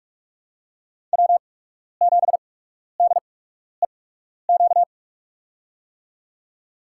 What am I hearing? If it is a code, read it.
Morse code, heard as a keyed tone, about 44 wpm, spaced as for 12 wpm, 690 Hz. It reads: W7DEQ
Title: USA Call Signs - 5 Characters - 8x Character Spacing - 45wpm